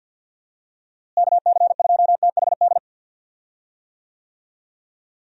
Morse code, heard as a keyed tone, about 50 wpm, 700 Hz. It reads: KC2THD
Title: USA Call Signs - 6 Characters - Repeated 3x - 50wpm